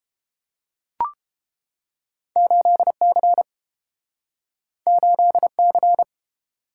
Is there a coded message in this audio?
Yes: Morse code from 2.4 to 6.0 s, heard as a keyed tone, about 29 wpm, 700 Hz. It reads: 8C 8C